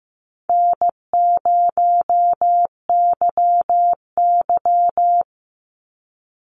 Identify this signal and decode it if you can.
Morse code, heard as a keyed tone, about 15 wpm, 700 Hz. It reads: N0YY
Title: Easy Call Signs Repeated 3x - 15wpm